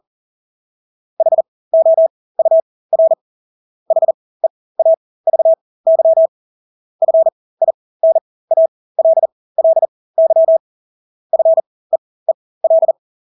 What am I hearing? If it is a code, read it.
Morse code, heard as a keyed tone, about 40 wpm, spaced as for 20 wpm, 650 Hz. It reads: HOUR HEAVY FINALLY FEEL
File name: Engelske ord 40-20 wpm